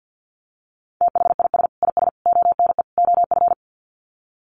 Morse code, heard as a keyed tone, about 50 wpm, 700 Hz. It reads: THIS IS ONE OF
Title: Sets of 4 Words 50wpm